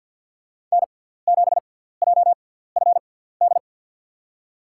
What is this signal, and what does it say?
Morse code, heard as a keyed tone, about 52 wpm, spaced as for 17 wpm, 700 Hz. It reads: N7JFD
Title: USA Call Signs - 5 Characters - 6x Character Spacing - 50wpm